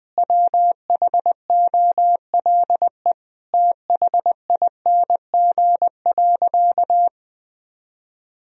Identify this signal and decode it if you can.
Morse code, heard as a keyed tone, about 21 wpm, 700 Hz. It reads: WHOLE THING.
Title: Sentences from Top 1000 Words - Review Part 1 - 20wpm